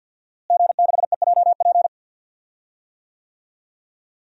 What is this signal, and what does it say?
Morse code, heard as a keyed tone, about 51 wpm, 700 Hz. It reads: G6EJP
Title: DXCC-100 Call Signs - 5 Characters - Repeated 2x - 50wpm